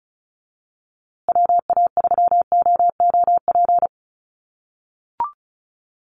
Morse code, heard as a keyed tone, about 34 wpm, 700 Hz. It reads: WA3OOP
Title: USA Call Signs Repeated 3x 35wpm